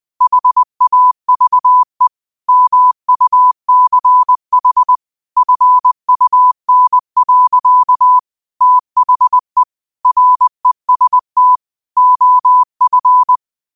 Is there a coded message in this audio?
Yes: Morse code from 0.2 to 13.4 s, heard as a keyed tone, about 20 wpm, 1000 Hz. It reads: HAVE MUCH FUN. THE REST OF